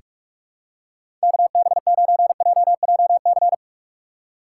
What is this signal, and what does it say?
Morse code, heard as a keyed tone, about 42 wpm, 700 Hz. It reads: KB9JJC
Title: USA Call Signs - 6 Characters - 3x Character Spacing - 45wpm